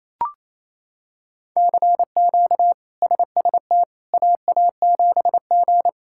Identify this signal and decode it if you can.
Morse code, heard as a keyed tone, about 27 wpm, 700 Hz. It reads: CQ SST AA7G
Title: K1USN SST QSO Practice - 28wpm